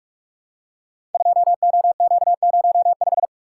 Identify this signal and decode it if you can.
Morse code, heard as a keyed tone, about 44 wpm, 700 Hz. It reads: 2OQ05
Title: Code Groups - 5 Letter:Numbers - 45wpm